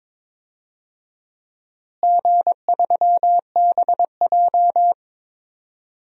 Morse code, heard as a keyed tone, about 22 wpm, 700 Hz. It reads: G3BJ
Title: DXCC-100 Call Signs - 4 Characters - Repeated 2x - 22wpm